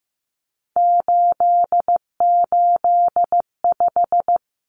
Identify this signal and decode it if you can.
Morse code, heard as a keyed tone, about 15 wpm, 700 Hz. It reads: 885
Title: Numbers - 3-Digit 15wpm